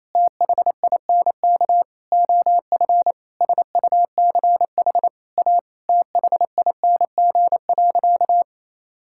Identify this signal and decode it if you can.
Morse code, heard as a keyed tone, about 28 wpm, 700 Hz. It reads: THINK OF SUCH A THING.
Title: Sentences from Top 1600 Words 28wpm